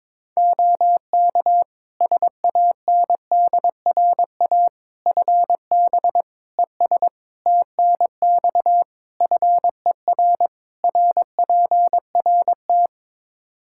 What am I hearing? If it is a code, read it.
Morse code, heard as a keyed tone, about 22 wpm, 700 Hz. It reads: OK SANDRA FB ES TNX FER RPRT
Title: Practice QSO - Exchange 2 - Info - 22wpm